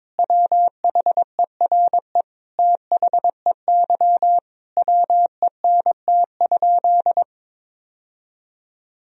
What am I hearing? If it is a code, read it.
Morse code, heard as a keyed tone, about 22 wpm, 700 Hz. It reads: WHERE THEY WENT?
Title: Sentences from Top 1000 Words - Review Part 3 - 22wpm